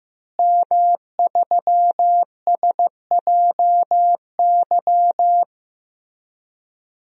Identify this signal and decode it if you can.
Morse code, heard as a keyed tone, about 15 wpm, 700 Hz. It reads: M3SJY
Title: DXCC-100 Call Signs - 5 Characters - Repeated 3x - 15wpm